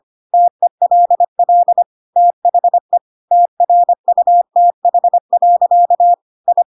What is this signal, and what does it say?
Morse code, heard as a keyed tone, about 25 wpm, 700 Hz. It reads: TELL THE TRUTH. I
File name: Pinocchio / Pinocchio ch8 25 25 ik1hge